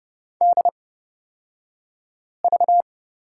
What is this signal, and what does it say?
Morse code, heard as a keyed tone, about 31 wpm, spaced as for 5 wpm, 700 Hz. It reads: DV
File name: Lesson 38d - Letter q - Two Characters 30wpm